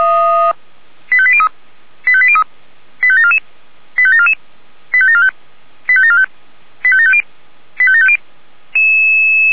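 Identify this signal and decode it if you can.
Five-tone selective call (ZVEI), heard as quick runs of five tones, tones of about 70 ms. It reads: 85803 85803 85744 85744 85745 85745 85750 85750
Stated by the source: ZVEI five-tone call 85726 received by SDR.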